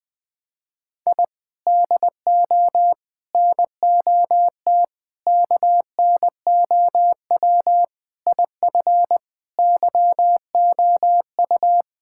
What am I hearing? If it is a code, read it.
Morse code, heard as a keyed tone, about 20 wpm, 700 Hz. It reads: I DO NOT KNOW IF YOU
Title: Sentences with 5-Word Phrases - Spoken Beginning - 20wpm